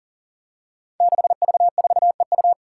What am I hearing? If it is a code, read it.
Morse code, heard as a keyed tone, about 40 wpm, 700 Hz. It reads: 6V4EU
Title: Five Challenging Letter-Number Combinations 40wpm